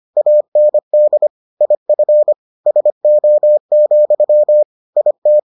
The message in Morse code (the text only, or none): AND IF SO, IT